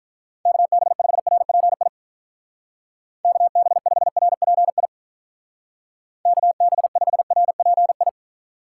Morse code, heard as a keyed tone, about 45 wpm, 700 Hz. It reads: KB5RPI KB5RPI KB5RPI